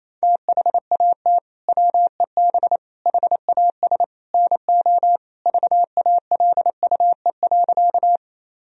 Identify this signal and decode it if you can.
Morse code, heard as a keyed tone, about 28 wpm, 700 Hz. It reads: THAT WEB HAS NO VALUE.
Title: Sentences from Top 1000 Words - Review Part 4 - 28wpm